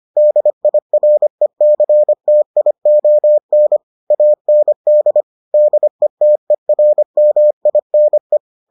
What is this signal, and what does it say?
Morse code, heard as a keyed tone, about 25 wpm, 600 Hz. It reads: DIRECTION AND DETERMINE